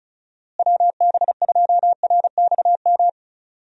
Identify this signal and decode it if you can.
Morse code, heard as a keyed tone, about 34 wpm, 700 Hz. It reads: WB2RXM
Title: USA Call Signs - 6 Characters - Repeated 2x - 35wpm